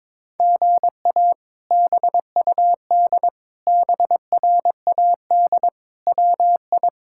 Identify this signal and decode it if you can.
Morse code, heard as a keyed tone, about 22 wpm, 700 Hz. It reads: GA BUD BRAD WI
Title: K1USN SST QSO Practice - 22wpm